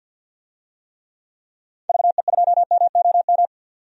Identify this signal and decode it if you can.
Morse code, heard as a keyed tone, about 48 wpm, 700 Hz. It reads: VE2MOM
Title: DXCC-100 Call Signs - 6 Characters - 50wpm